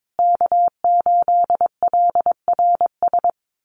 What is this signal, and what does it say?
Morse code, heard as a keyed tone, about 22 wpm, 700 Hz. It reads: K8LRS
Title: USA Call Signs Repeated 2x 22wpm